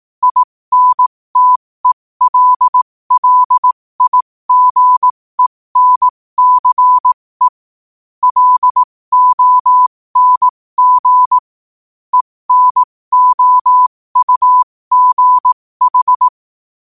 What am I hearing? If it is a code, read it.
Morse code, heard as a keyed tone, about 18 wpm, spaced as for 15 wpm, 1000 Hz. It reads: INTELLIGENCE LONG ENOUGH
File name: qotdfiles / 15wpm